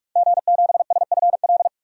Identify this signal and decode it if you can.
Morse code, heard as a keyed tone, about 46 wpm, 700 Hz. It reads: G7SFL
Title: DXCC-100 Call Signs - 5 Characters - 45wpm